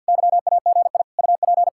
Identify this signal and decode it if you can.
Morse code, heard as a keyed tone, about 64 wpm, 700 Hz. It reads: YAGI UP